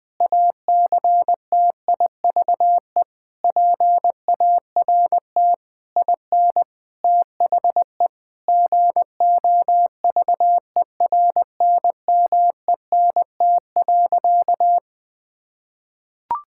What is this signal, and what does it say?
Morse code, heard as a keyed tone, about 20 wpm, 700 Hz. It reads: ACTIVE PART IN THE GOVERNMENT.
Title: Sentences from Top 1000 Words - Review Part 6 - 20wpm